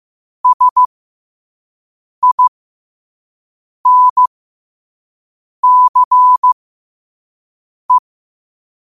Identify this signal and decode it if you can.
Morse code, heard as a keyed tone, about 15 wpm, spaced as for 5 wpm, 1000 Hz. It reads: SINCE